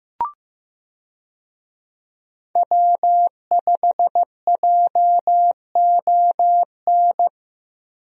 Morse code, heard as a keyed tone, about 15 wpm, 700 Hz. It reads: W5JON
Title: USA Call Signs - 5 Characters - Repeated 2x - 15wpm